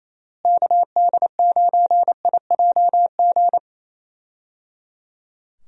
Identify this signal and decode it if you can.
Morse code, heard as a keyed tone, about 28 wpm, 700 Hz. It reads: KD9IJG
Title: USA Call Signs - 6 Characters - Repeated 3x - 28wpm